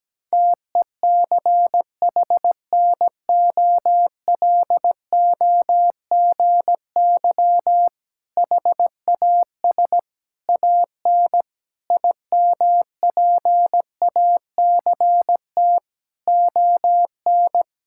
Morse code, heard as a keyed tone, about 17 wpm, 700 Hz. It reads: TECHNOLOGY HAS AN IMPACT ON
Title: Sentences from Top 1000 Words - Review Part 6 - 17wpm